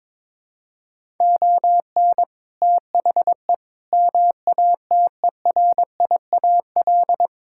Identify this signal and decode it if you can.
Morse code, heard as a keyed tone, about 22 wpm, 700 Hz. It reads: ON THE MATERIAL
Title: Sentences from Top 1000 Words - Review Part 5 - 22wpm